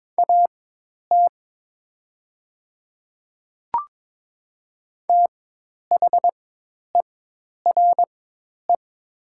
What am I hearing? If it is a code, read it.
Morse code, heard as a keyed tone, about 22 wpm, spaced as for 10 wpm, 700 Hz. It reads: AT THERE
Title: 4x Character Spacing - Top 100 Words - 22wpm